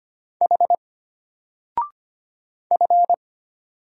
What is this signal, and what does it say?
Morse code, heard as a keyed tone, about 25 wpm, 700 Hz. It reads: H F
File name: Single-Letter-Number-25wpm